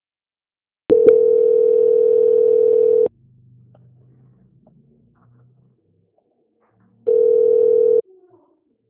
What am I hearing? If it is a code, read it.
Ringback tone (North America), 440 + 480 Hz.
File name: out- -2051- - - .52